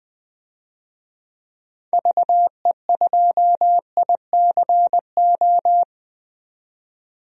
Morse code, heard as a keyed tone, about 20 wpm, 700 Hz. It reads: VE2ICO
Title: DXCC-100 Call Signs - 6 Characters - 20wpm